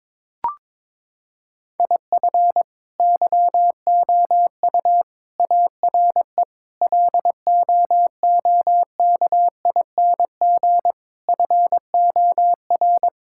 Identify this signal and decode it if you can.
Morse code, heard as a keyed tone, about 22 wpm, 700 Hz. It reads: IF YOU ARE LOOKING FOR
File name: practice / 22wpm / Sets of 6 Words - 22wpm